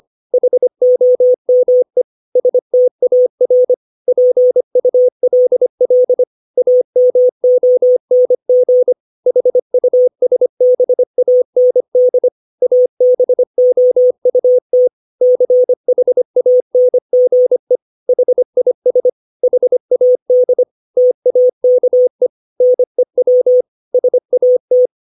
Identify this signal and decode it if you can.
Morse code, heard as a keyed tone, about 25 wpm, 500 Hz. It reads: HOME STAR PULL AMONG HUSBAND ABOUT CHANGE HIS HAD TAKE NEW SAT
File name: 632 words / 632 Words 25WPM